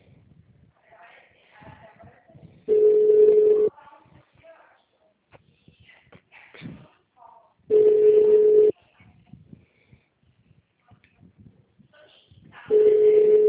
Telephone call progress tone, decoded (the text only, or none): ringback tone